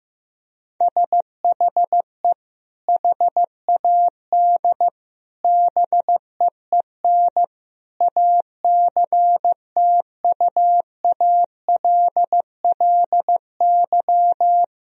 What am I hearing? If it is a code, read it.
Morse code, heard as a keyed tone, about 15 wpm, 700 Hz. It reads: SHE HAD BEEN ACTUALLY
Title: Sentences from Top 1000 Words - Review Part 4 - 15wpm